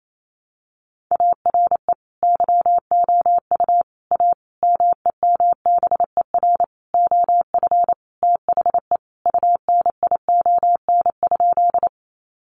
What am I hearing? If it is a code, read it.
Morse code, heard as a keyed tone, about 28 wpm, 700 Hz. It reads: ARE YOU A MEMBER OF THE UNION?
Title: Sentences from Top 1400 Words 28wpm